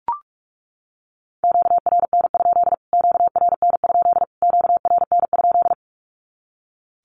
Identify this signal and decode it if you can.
Morse code, heard as a keyed tone, about 44 wpm, 700 Hz. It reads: QRN? QRN? QRN?
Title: Q-Codes Repeated 3x 45wpm